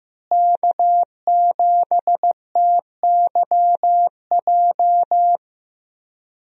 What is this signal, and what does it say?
Morse code, heard as a keyed tone, about 15 wpm, 700 Hz. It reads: K7TYJ
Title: USA Call Signs - 5 Characters - Repeated 2x - 15wpm